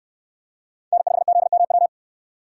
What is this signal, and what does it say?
Morse code, heard as a keyed tone, about 68 wpm, 700 Hz. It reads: N5ZMW